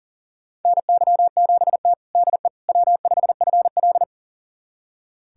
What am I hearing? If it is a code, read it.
Morse code, heard as a keyed tone, about 40 wpm, 700 Hz. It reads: NY7T DE W5FL
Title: USA Callsign DE Callsign - 4 Characters - 40wpm